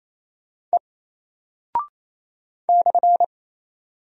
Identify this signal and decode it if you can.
Morse code, heard as a keyed tone, about 27 wpm, 700 Hz. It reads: E /
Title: Single Character (Letter:Number:Punctuation:Prosign) Alt 28wpm